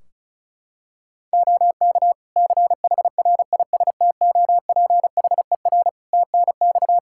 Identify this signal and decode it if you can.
Morse code, heard as a keyed tone, about 34 wpm, 700 Hz. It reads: OK CHRISTOPHER TNX
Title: Practice QSO - Exchange 4 - Ending - 35wpm